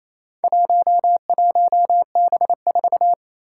Morse code, heard as a keyed tone, about 28 wpm, 700 Hz. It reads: 11B4
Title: Four Challenging Letter-Number Combinations 28wpm